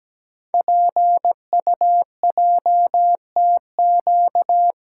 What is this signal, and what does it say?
Morse code, heard as a keyed tone, about 17 wpm, 700 Hz. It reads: PUJTQ
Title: Code Groups - 5 Letter:Numbers - 17wpm